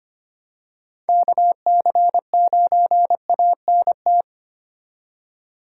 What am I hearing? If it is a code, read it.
Morse code, heard as a keyed tone, about 25 wpm, 700 Hz. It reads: KC9ANT
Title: USA Call Signs - 6 Characters - 4x Character Spacing - 25wpm